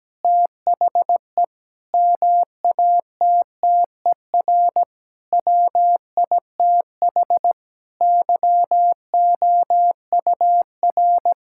Morse code, heard as a keyed tone, about 17 wpm, 700 Hz. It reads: THE MATTER WITH YOUR
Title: Sentences from Top 1000 Words - Review Part 5 - 17wpm